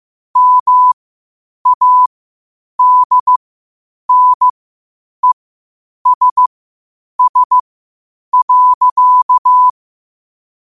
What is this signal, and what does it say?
Morse code, heard as a keyed tone, about 15 wpm, spaced as for 8 wpm, 1000 Hz. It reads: MADNESS.
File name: qotdfiles / 8wpm